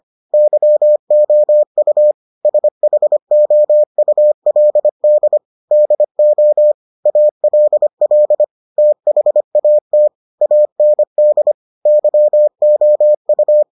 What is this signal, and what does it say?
Morse code, heard as a keyed tone, about 25 wpm, 600 Hz. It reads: YOU SHOULD DO ALL THAT AND YOU